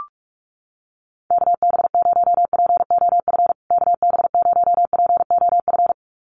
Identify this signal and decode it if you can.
Morse code, heard as a keyed tone, about 44 wpm, 700 Hz. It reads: KB0POF KB0POF
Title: USA Call Signs Repeated 2x 45wpm